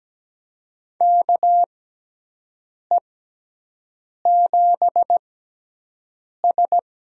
Morse code, heard as a keyed tone, about 17 wpm, spaced as for 6 wpm, 700 Hz. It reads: KE7S
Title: USA Call Signs - 6 Characters - 6x Character Spacing - 17wpm